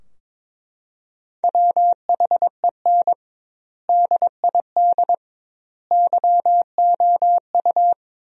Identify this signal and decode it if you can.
Morse code, heard as a keyed tone, about 21 wpm, 700 Hz. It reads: WHEN DID YOU
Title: 2x Word-Spacing - Sentences from Top 300 Words - 22wpm